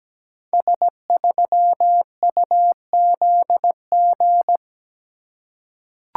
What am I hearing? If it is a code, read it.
Morse code, heard as a keyed tone, about 17 wpm, 700 Hz. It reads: S3UZG
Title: Code Groups - 5 Letter:Numbers - 17wpm